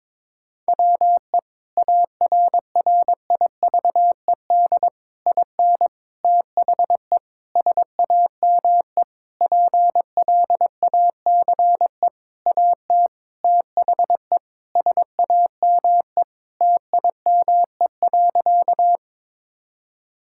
Morse code, heard as a keyed tone, about 21 wpm, 700 Hz. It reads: WE ARRIVED IN THE SAME PLACE AT THE SAME TIME.
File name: practice / 22wpm / Sentences with 8-Word Phrases - 22wpm